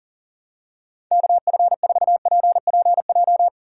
Morse code, heard as a keyed tone, about 40 wpm, 700 Hz. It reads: KF4PPJ
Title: USA Call Signs - 6 Characters - 3x Character Spacing - 40wpm